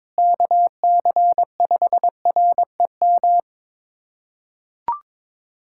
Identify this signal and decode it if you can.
Morse code, heard as a keyed tone, about 22 wpm, 700 Hz. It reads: KC5REM
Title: USA Callsign DE Callsign 22wpm